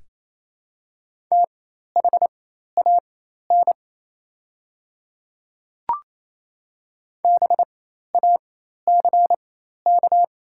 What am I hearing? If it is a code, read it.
Morse code, heard as a keyed tone, about 28 wpm, spaced as for 13 wpm, 700 Hz. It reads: THAN BACK
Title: 4x Character Spacing - Top 200 Words - 28wpm